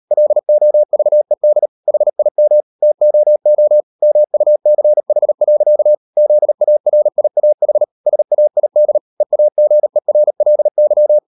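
Morse code, heard as a keyed tone, about 38 wpm, 600 Hz. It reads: LOVED HIM TOO MUCH. ZARIAH SAID EAGERLY
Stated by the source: Morse code practice at 38 WPM, sending Ken Follett's novel Never in English, from CW med Gnister.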